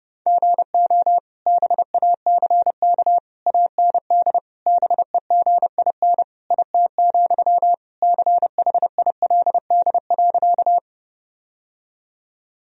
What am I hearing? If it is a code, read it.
Morse code, heard as a keyed tone, about 30 wpm, 700 Hz. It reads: GO BACK AND BEGIN IT, CHILD.
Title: Sentences from Top 1000 Words - Review Part 1 - 30wpm